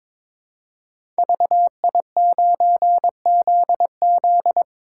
Morse code, heard as a keyed tone, about 22 wpm, 700 Hz. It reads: VI9ZZ
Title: Code Groups - 5 Letter:Numbers - 22wpm